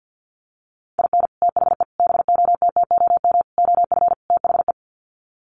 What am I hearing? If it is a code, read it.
Morse code, heard as a keyed tone, about 50 wpm, 700 Hz. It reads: IN THE BOTTOM OF THE
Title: Sets of 5 Words Encore 50wpm